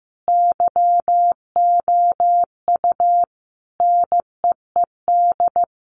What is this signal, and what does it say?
Morse code, heard as a keyed tone, about 15 wpm, 700 Hz. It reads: YOU NEED
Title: Sentences from Top 1500 Words 15wpm